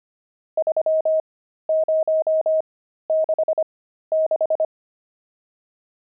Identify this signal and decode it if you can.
Morse code, heard as a keyed tone, about 25 wpm, spaced as for 13 wpm, 630 Hz. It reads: 3066
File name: pangram-fs-13-cs-25-nums16